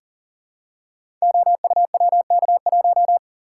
Morse code, heard as a keyed tone, about 40 wpm, 700 Hz. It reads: OUWK1
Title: Code Groups - 5 Letter:Numbers - 40wpm